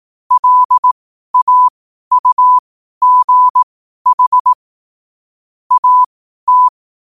Morse code, heard as a keyed tone, about 18 wpm, spaced as for 12 wpm, 1000 Hz. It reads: LAUGH AT
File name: qotdfiles / 12wpm